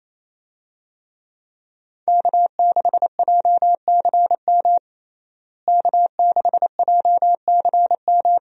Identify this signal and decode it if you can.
Morse code, heard as a keyed tone, about 28 wpm, 700 Hz. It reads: K6JCM K6JCM
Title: USA Call Signs - 5 Characters - Repeated 2x - 28wpm